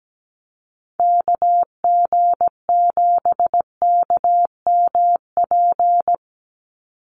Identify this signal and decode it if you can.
Morse code, heard as a keyed tone, about 17 wpm, 700 Hz. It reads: KG7KMP
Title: USA Call Signs 17wpm